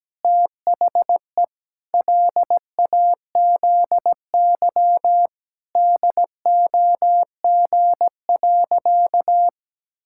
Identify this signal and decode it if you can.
Morse code, heard as a keyed tone, about 17 wpm, 700 Hz. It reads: THE LAZY DOG.
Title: Pangram Challenge v2 - 17wpm